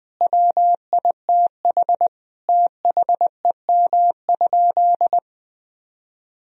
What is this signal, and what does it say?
Morse code, heard as a keyed tone, about 20 wpm, 700 Hz. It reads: WITH THEM?
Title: Sentences from Top 1000 Words - Review Part 1 - 20wpm